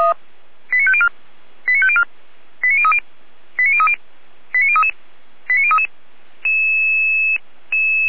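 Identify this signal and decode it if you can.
Five-tone selective call (ZVEI), heard as quick runs of five tones, tones of about 70 ms. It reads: 89504 89504 89030 89030 89033 89033
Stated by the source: ZVEI five-tone selective call 89010 received on SDR.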